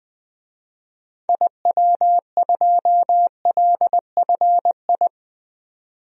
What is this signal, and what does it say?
Morse code, heard as a keyed tone, about 21 wpm, 700 Hz. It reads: IW2LFI
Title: DXCC-100 Call Signs - 6 Characters - 20wpm